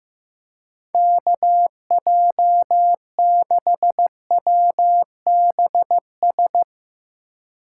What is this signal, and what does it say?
Morse code, heard as a keyed tone, about 15 wpm, 700 Hz. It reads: KJ6WBS
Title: USA Call Signs - 4x Character Spacing - 15wpm